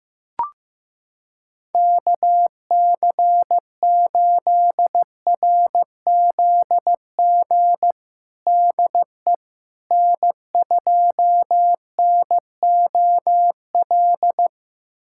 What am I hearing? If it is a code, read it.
Morse code, heard as a keyed tone, about 15 wpm, 700 Hz. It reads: KC8RZG DE N2NOL
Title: USA Callsign DE Callsign 15wpm